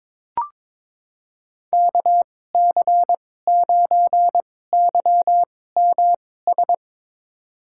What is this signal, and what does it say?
Morse code, heard as a keyed tone, about 22 wpm, spaced as for 16 wpm, 700 Hz. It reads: KC9YMS
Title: USA Call Signs - 2x Character Spacing - 22wpm